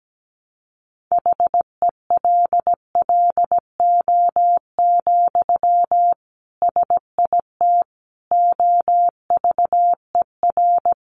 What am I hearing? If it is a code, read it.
Morse code, heard as a keyed tone, about 17 wpm, 700 Hz. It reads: HELLO, SIT OVER
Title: Sentences from Top 1100 Words 17wpm